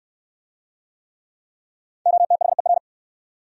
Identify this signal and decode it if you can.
Morse code, heard as a keyed tone, about 68 wpm, 700 Hz. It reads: OTHER